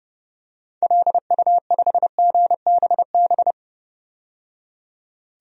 Morse code, heard as a keyed tone, about 31 wpm, 700 Hz. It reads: LU5GBB
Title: DXCC-100 Call Signs - 6 Characters - 30wpm